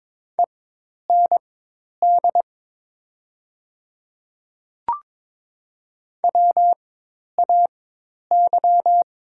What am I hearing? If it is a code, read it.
Morse code, heard as a keyed tone, about 22 wpm, spaced as for 10 wpm, 700 Hz. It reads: END WAY
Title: 4x Character Spacing - Top 200 Words - 22wpm